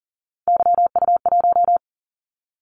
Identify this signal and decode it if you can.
Morse code, heard as a keyed tone, about 40 wpm, 700 Hz. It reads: YU1
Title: Callsign Prefixes for Top 25 DXCC - 40wpm